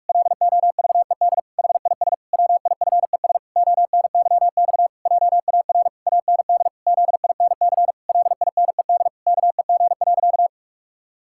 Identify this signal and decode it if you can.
Morse code, heard as a keyed tone, about 46 wpm, 700 Hz. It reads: LOVED HIS WIFES ONYX JAR AND ZINC LINED KEG.
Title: Pangram Challenge v2 - 45wpm